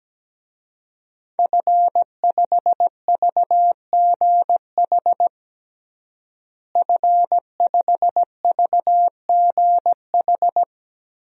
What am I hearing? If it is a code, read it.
Morse code, heard as a keyed tone, about 17 wpm, 700 Hz. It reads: F5VGH F5VGH
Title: DXCC-100 Call Signs - 5 Characters - Repeated 3x - 17wpm